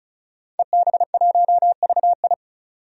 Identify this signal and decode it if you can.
Morse code, heard as a keyed tone, about 36 wpm, 700 Hz. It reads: EB1VI
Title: DXCC-100 Call Signs - 5 Characters - 35wpm